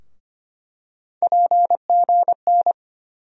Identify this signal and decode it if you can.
Morse code, heard as a keyed tone, about 25 wpm, 700 Hz. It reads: PGN
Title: USA Call Sign Postfixes - 3 Letters 25wpm